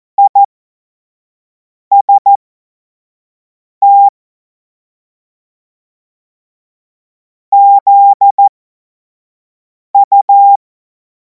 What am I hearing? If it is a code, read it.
Morse code, heard as a keyed tone, about 14 wpm, spaced as for 5 wpm, 800 Hz. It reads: IST ZU
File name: Morsekurs / Witze / Witz 10. März 2026 25 F70